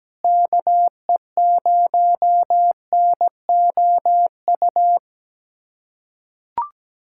Morse code, heard as a keyed tone, about 17 wpm, 700 Hz. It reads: KE0NOU
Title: USA Callsign DE Callsign - 6 Characters - 17wpm